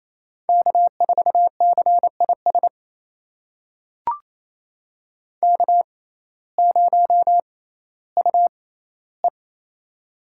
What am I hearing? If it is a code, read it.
Morse code, heard as a keyed tone, about 28 wpm, spaced as for 15 wpm, 700 Hz. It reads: K4CIS K0UE
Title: USA Call Signs - 6x Character Spacing - 28wpm